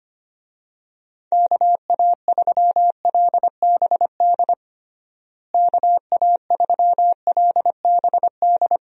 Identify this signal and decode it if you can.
Morse code, heard as a keyed tone, about 25 wpm, 700 Hz. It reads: KA3LBD KA3LBD
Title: USA Call Signs - 6 Characters - Repeated 2x - 25wpm